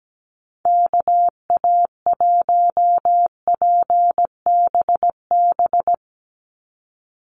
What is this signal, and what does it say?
Morse code, heard as a keyed tone, about 17 wpm, 700 Hz. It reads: KA1PBB
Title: USA Call Signs 17wpm